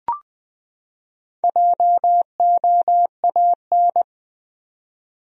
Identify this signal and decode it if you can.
Morse code, heard as a keyed tone, about 20 wpm, 700 Hz. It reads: JOAN
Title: USA Common Names - Max 5 Letters - 20wpm